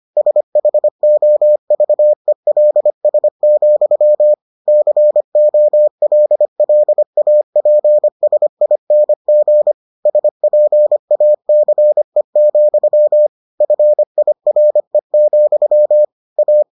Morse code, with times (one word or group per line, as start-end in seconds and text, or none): 0.2-4.3 SHOVELS,
4.7-9.7 COLLAPSING
10.1-13.3 SPACE,
13.6-16.0 FIRE,
16.4-16.6 A